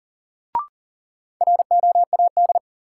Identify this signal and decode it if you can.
Morse code, heard as a keyed tone, about 40 wpm, 700 Hz. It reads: ROAD
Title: Sentences from Top 400 Words - New Word Drill - 40wpm